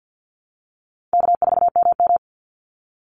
Morse code, heard as a keyed tone, about 49 wpm, 700 Hz. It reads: K4MM
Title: Call Signs Repeated 3x - 50wpm